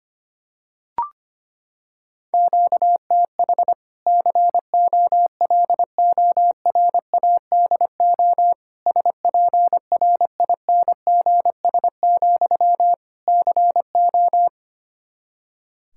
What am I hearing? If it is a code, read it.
Morse code, heard as a keyed tone, about 25 wpm, 700 Hz. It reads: QTH COLORADO SPRINGS, CO
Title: Practice QSO - Exchange 1 - Beginning - 25wpm